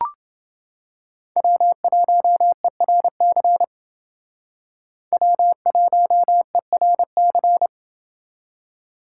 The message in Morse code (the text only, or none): W1ERC W1ERC